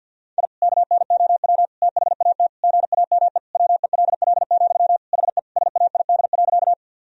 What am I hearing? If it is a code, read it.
Morse code, heard as a keyed tone, about 50 wpm, 700 Hz. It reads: I KNOW THAT GAME WELL, HE SAID.
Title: Sentences from Top 1000 Words - Review Part 3 - 50wpm